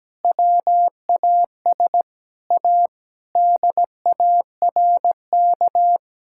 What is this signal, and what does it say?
Morse code, heard as a keyed tone, about 17 wpm, 700 Hz. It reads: WAS A DARK
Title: Sentences from Top 1000 Words - Review Part 5 - 17wpm